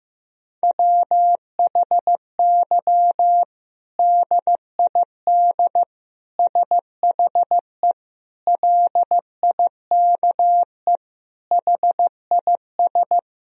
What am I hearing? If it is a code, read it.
Morse code, heard as a keyed tone, about 15 wpm, 700 Hz. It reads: WHY DID SHE LIKE HIS
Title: Sentences from Top 1000 Words - Review Part 3 - 15wpm